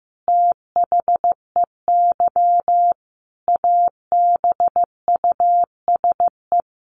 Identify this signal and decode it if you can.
Morse code, heard as a keyed tone, about 15 wpm, 700 Hz. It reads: THEY ABUSE
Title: Sentences from Top 1400 Words 15wpm